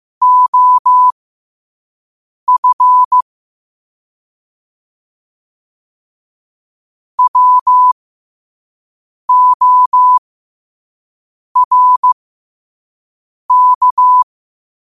Morse code, heard as a keyed tone, about 15 wpm, spaced as for 5 wpm, 1000 Hz. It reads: OF WORK
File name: qotdfiles / 5wpm